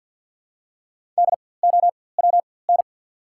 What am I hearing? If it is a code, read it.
Morse code, heard as a keyed tone, about 53 wpm, spaced as for 24 wpm, 700 Hz. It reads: DOWN
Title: 4x Character Spacing - Top 100 Words - 50wpm